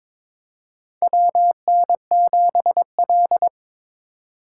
Morse code, heard as a keyed tone, about 23 wpm, 700 Hz. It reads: WN7L
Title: USA Call Signs - 4 Characters - 2x Character Spacing - 22wpm